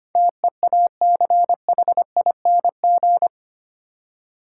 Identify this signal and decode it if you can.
Morse code, heard as a keyed tone, about 25 wpm, 700 Hz. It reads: TEACHING
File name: Eight Letter Words - 25wpm